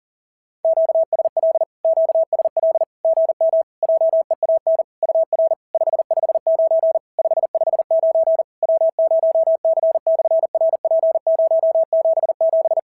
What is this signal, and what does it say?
Morse code, heard as a keyed tone, about 43 wpm, 650 Hz. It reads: QSL QSL GM JEAN UR 559 559 W0C/RP077
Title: SOTA QSO Practice 40wpm